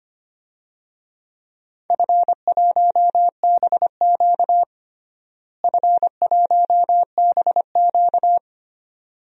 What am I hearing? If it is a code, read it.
Morse code, heard as a keyed tone, about 25 wpm, 700 Hz. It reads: F1BQ F1BQ
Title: DXCC-100 Call Signs - 4 Characters - Repeated 2x - 25wpm